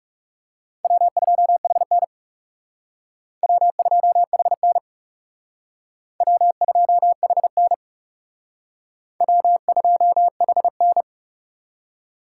Morse code, heard as a keyed tone, about 37 wpm, 700 Hz. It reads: W2HN W2HN W2HN W2HN
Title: Speed Racing - USA Call Signs 30wpm